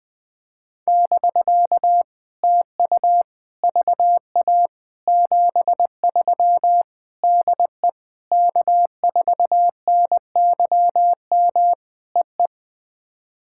Morse code, heard as a keyed tone, about 20 wpm, 700 Hz. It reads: <BK> TU VA 73 DE K4NYM EE
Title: POTA QSO Practice - 20wpm